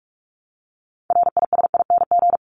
Morse code, heard as a keyed tone, about 44 wpm, 700 Hz. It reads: RISING
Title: ING Suffix - Top 100 Words with Base Words 45wpm